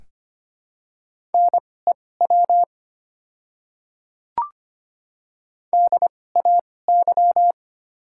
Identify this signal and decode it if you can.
Morse code, heard as a keyed tone, about 25 wpm, spaced as for 18 wpm, 700 Hz. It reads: NEW DAY
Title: 2x Character Spacing - Top 300 Words - 25wpm